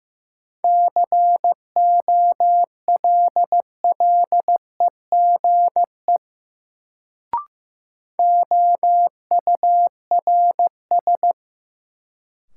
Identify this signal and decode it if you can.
Morse code, heard as a keyed tone, about 15 wpm, 700 Hz. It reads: COLLEGE OURS
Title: Sentences from Top 1300 Words - New Word Drill Alt - 15wpm